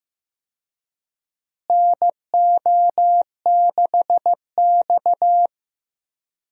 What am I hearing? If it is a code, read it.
Morse code, heard as a keyed tone, about 15 wpm, 700 Hz. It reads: NO6X
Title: USA Call Signs - 4 Characters - Repeated 3x - 15wpm